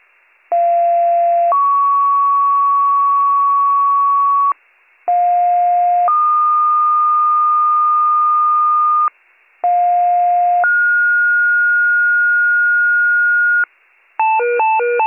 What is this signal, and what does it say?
Two-tone page, heard as a one-second tone, then a three-second tone, about 690 then 1120 Hz, about 710 then 1220 Hz, about 710 then 1430 Hz. It is heard from 0.5 to 13.6 s.